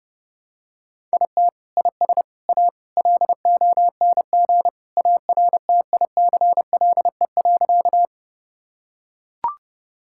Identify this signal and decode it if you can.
Morse code, heard as a keyed tone, about 32 wpm, 700 Hz. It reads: IT IS A LONG ARTICLE.
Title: Sentences from Top 1000 Words - Review Part 5 - 30wpm